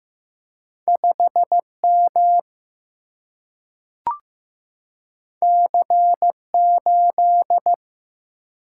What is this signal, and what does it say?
Morse code, heard as a keyed tone, about 15 wpm, 700 Hz. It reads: 5M C8
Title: Single-Letter Single-Number 15wpm